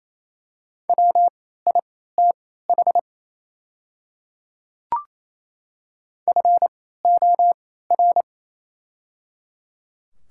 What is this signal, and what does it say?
Morse code, heard as a keyed tone, about 28 wpm, spaced as for 16 wpm, 700 Hz. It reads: WITH FOR
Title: 3x Character Spacing - Top 100 Words - 28wpm